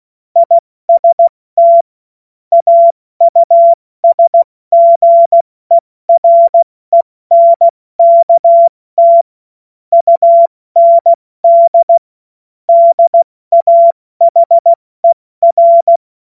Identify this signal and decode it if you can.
Morse code, heard as a keyed tone, about 16 wpm, 680 Hz. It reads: IST AUSGERENKT UND DAHER
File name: Morsekurs / Witze / Witz 13. Dezember 2022 70 F80